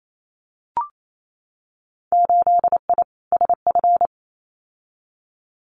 Morse code, heard as a keyed tone, about 28 wpm, 700 Hz. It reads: 8I SF
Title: Winter Field Day Exchanges Encore 28wpm